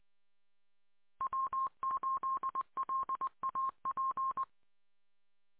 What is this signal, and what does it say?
Morse code, heard as a keyed tone, about 20 wpm, 1060 Hz. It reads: W8LAP